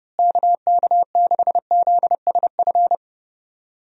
Morse code, heard as a keyed tone, about 30 wpm, 700 Hz. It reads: KK6ZSF
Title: USA Call Signs - 6 Characters - Repeated 2x - 30wpm